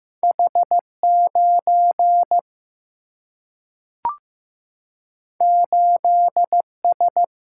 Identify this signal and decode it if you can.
Morse code, heard as a keyed tone, about 15 wpm, 700 Hz. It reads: H 9 8 S
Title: Single-Letter Single-Number 15wpm